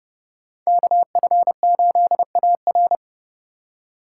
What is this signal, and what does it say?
Morse code, heard as a keyed tone, about 30 wpm, 700 Hz. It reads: KF8AR